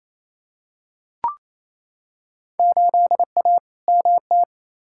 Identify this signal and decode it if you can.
Morse code, heard as a keyed tone, about 28 wpm, 700 Hz. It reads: 8A MT
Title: Field Day Exchanges Encore - 28wpm